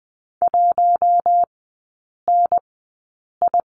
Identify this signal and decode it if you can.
Morse code, heard as a keyed tone, about 20 wpm, spaced as for 8 wpm, 700 Hz. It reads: 1NI
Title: 047 Lesson 9e - Number 4 - Three Characters 20wpm